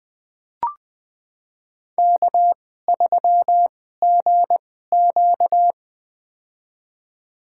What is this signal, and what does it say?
Morse code, heard as a keyed tone, about 20 wpm, spaced as for 14 wpm, 700 Hz. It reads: K3GQ
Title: USA Call Signs - 4 Characters - 2x Character Spacing - 20wpm